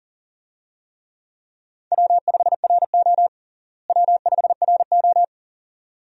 Morse code, heard as a keyed tone, about 37 wpm, 700 Hz. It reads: W5RO W5RO
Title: USA Call Signs - 4 Characters - Repeated 2x - 40wpm